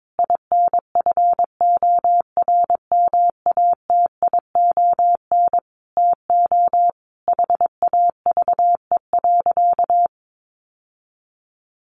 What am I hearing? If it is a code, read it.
Morse code, heard as a keyed tone, about 22 wpm, 700 Hz. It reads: INFORMATION TO HAVE.
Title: Sentences from Top 1400 Words 22wpm